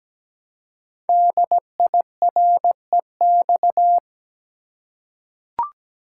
Morse code, heard as a keyed tone, about 17 wpm, 700 Hz. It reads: DIREX
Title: Code Groups - 5 Letters - 17wpm